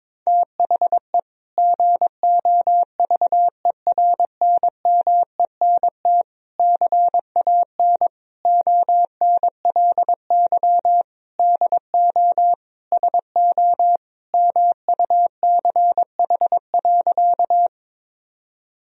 Morse code, heard as a keyed tone, about 21 wpm, 700 Hz. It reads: THE GOVERNMENT CAN ONLY DO SO MUCH.
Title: Sentences from Top 1000 Words - Review Part 3 - 22wpm